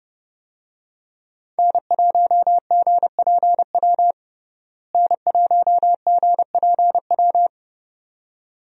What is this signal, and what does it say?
Morse code, heard as a keyed tone, about 31 wpm, 700 Hz. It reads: N1GPW N1GPW
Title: USA Call Signs - 5 Characters - Repeated 2x - 30wpm